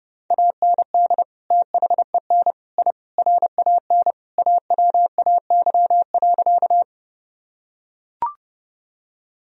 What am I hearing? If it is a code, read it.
Morse code, heard as a keyed tone, about 30 wpm, 700 Hz. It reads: AND THEN I RAN AWAY.
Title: Sentences from Top 1000 Words - Review Part 3 - 30wpm